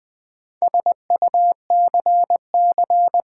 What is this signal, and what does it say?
Morse code, heard as a keyed tone, about 20 wpm, 700 Hz. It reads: SUCC
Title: Build A Word - 4 Letters At A Time - 10-Letter Words - 20wpm